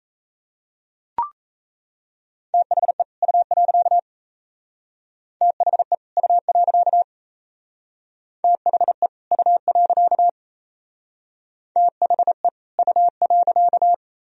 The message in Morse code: THE U. THE U. THE U. THE U.

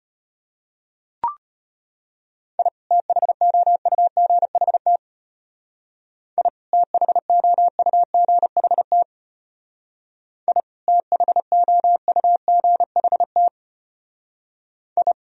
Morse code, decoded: I THOUGHT I THOUGHT I THOUGHT I

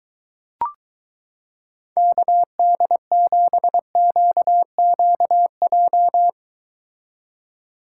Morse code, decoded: KD7QQJ

